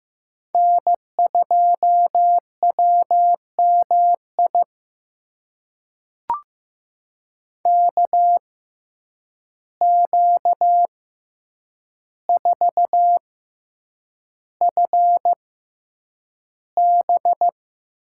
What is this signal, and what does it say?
Morse code, heard as a keyed tone, about 15 wpm, spaced as for 8 wpm, 700 Hz. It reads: N2WMI KQ4FB
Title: USA Call Signs - 5 Characters - 6x Character Spacing - 15wpm